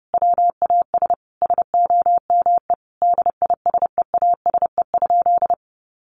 Morse code, heard as a keyed tone, about 30 wpm, 700 Hz. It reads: WAS SOME DISEASE?
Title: Sentences from Top 1500 Words 30wpm